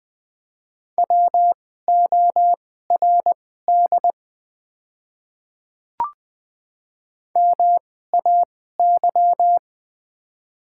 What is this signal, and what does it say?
Morse code, heard as a keyed tone, about 20 wpm, spaced as for 14 wpm, 700 Hz. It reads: WORD MAY
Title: 2x Character Spacing - Top 200 Words - 20wpm